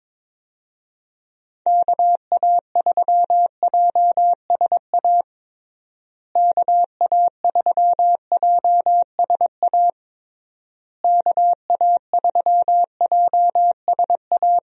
Morse code, heard as a keyed tone, about 21 wpm, 700 Hz. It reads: KA3JSA KA3JSA KA3JSA
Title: USA Call Signs - 6 Characters - Repeated 3x - 22wpm